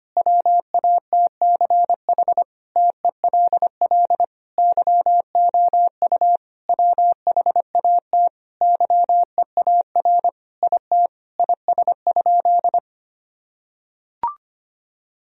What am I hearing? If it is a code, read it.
Morse code, heard as a keyed tone, about 25 wpm, 700 Hz. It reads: WATCH TELL YOU WHAT YEAR IT IS?
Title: Sentences from Top 1000 Words - Review Part 1 - 25wpm